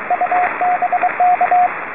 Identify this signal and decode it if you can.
Morse code, heard as a keyed tone, about 23 wpm, 720 Hz, with radio noise. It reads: UBK